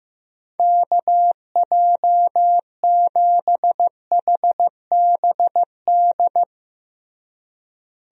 Morse code, heard as a keyed tone, about 15 wpm, 700 Hz. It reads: KJ7HBD
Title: USA Call Signs - 6 Characters - Repeated 2x - 15wpm